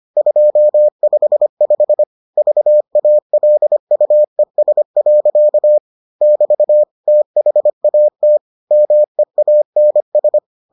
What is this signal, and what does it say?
Morse code, heard as a keyed tone, about 25 wpm, 600 Hz. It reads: 255 VALUES. <BT> THAT MEANS